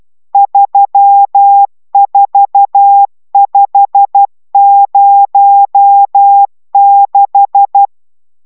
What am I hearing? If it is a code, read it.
Morse code, heard as a keyed tone, about 12 wpm, 800 Hz. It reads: 34506